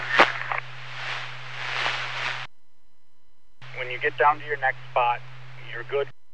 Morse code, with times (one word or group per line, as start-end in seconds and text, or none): none